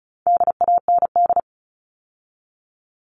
Morse code, heard as a keyed tone, about 34 wpm, 700 Hz. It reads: DAND
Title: All Cut Numbers - 4-Digit 35wpm